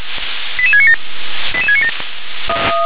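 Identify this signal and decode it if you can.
Five-tone selective call (ZVEI), heard as quick runs of five tones, tones of about 70 ms. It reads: 88697 88697
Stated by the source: ZVEI five-tone selective call received by SDR.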